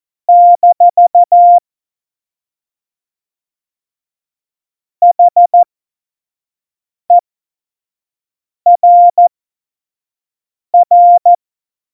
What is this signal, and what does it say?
Morse code, heard as a keyed tone, about 14 wpm, spaced as for 5 wpm, 700 Hz. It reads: - HERR